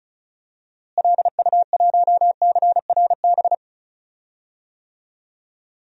Morse code, heard as a keyed tone, about 34 wpm, 700 Hz. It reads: LU1CRB
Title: DXCC-100 Call Signs - 6 Characters - Repeated 2x - 35wpm